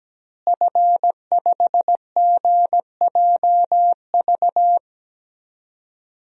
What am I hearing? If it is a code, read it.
Morse code, heard as a keyed tone, about 17 wpm, 700 Hz. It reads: F5GJV
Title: DXCC-100 Call Signs - 5 Characters - Repeated 3x - 17wpm